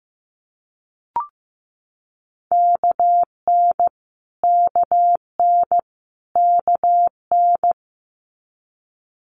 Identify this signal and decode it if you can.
Morse code, heard as a keyed tone, about 15 wpm, 700 Hz. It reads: KN KN KN
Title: Multi-letter Phonemes Repeated 3x 15wpm